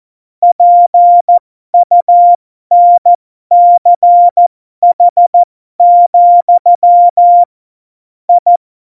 Morse code, heard as a keyed tone, about 14 wpm, spaced as for 12 wpm, 700 Hz. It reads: PUNCH, I
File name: Morsekurs / Witze